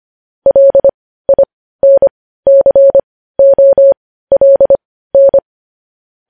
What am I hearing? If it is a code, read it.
Morse code, heard as a keyed tone, about 25 wpm, spaced as for 15 wpm, 550 Hz. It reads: LINCOLN
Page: news headlines in morse code at 15 wpm